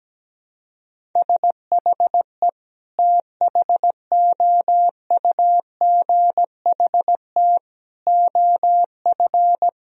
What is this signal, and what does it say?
Morse code, heard as a keyed tone, about 17 wpm, 700 Hz. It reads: SHE THOUGHT OF
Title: Sentences from Top 1000 Words - Review Part 1 - 17wpm